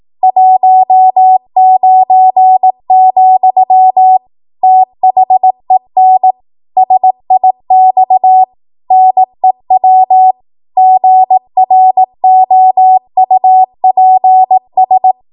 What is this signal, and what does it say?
Morse code, heard as a keyed tone, about 18 wpm, 750 Hz. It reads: 19, THEN SIX NEW GROUPS
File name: Archive / 18 WPM / 18WPM